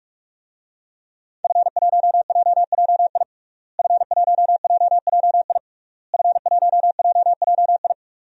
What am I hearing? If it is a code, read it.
Morse code, heard as a keyed tone, about 46 wpm, 700 Hz. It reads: F1JJI F1JJI F1JJI
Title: DXCC-100 Call Signs - 5 Characters - Repeated 3x - 45wpm